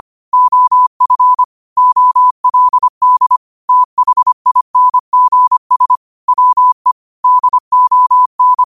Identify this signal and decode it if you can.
Morse code, heard as a keyed tone, about 25 wpm, 1000 Hz. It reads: OF OLD THINGS WE DON
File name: qotdfiles / 25wpm